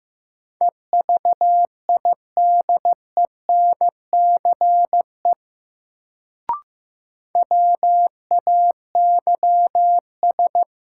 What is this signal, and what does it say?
Morse code, heard as a keyed tone, about 15 wpm, 700 Hz. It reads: EVIDENCE WAYS
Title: Sets of 5 Words Encore - New Word Drill - 15wpm